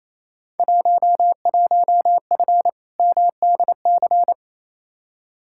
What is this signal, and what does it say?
Morse code, heard as a keyed tone, about 28 wpm, 700 Hz. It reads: 11F MDC
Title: Field Day Exchanges Encore - 28wpm